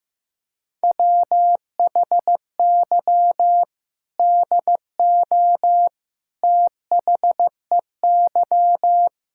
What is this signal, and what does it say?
Morse code, heard as a keyed tone, about 16 wpm, 700 Hz. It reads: WHY DO THEY
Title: Sentences from Top 1600 Words 15wpm